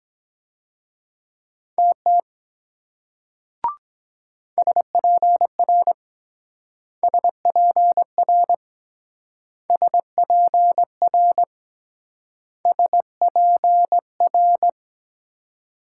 Morse code, heard as a keyed tone, about 23 wpm, 700 Hz. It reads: TT SPR SPR SPR SPR